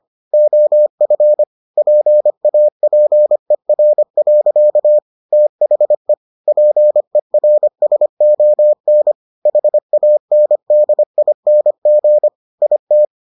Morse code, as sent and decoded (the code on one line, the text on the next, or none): --- ..-. / .--. .- .--. .-. .-.-.- / - .... / .--. .-. ... --- -. / .... .- -. -.. .. -. --. / .. -
OF PAPER. THE PERSON HANDING IT